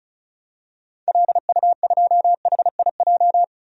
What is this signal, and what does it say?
Morse code, heard as a keyed tone, about 36 wpm, 700 Hz. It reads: LU2HIJ
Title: DXCC-100 Call Signs - 6 Characters - 35wpm